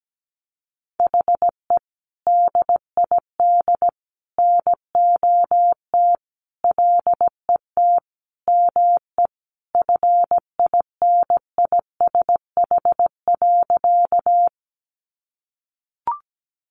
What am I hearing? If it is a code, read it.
Morse code, heard as a keyed tone, about 17 wpm, 700 Hz. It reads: HE DID NOT LET ME FINISH.
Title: Sentences from Top 1200 Words 17wpm